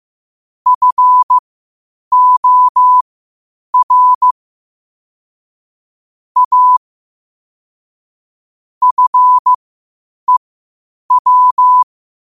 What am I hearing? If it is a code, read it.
Morse code, heard as a keyed tone, about 15 wpm, spaced as for 8 wpm, 1000 Hz. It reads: FOR A FEW
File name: qotdfiles / 8wpm